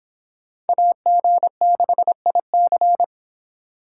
Morse code, heard as a keyed tone, about 25 wpm, 700 Hz. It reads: AG6IC